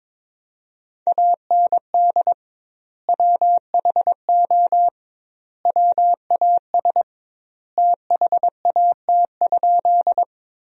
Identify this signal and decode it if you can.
Morse code, heard as a keyed tone, about 21 wpm, 700 Hz. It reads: AND WHO WAS THAT?
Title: 2x Word-Spacing - Sentences from Top 100 Words - 22wpm